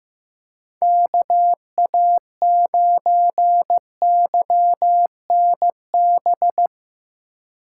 Morse code, heard as a keyed tone, about 17 wpm, 700 Hz. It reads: KA9YNB
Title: Speed Racing - USA 2x3 Call Signs 15wpm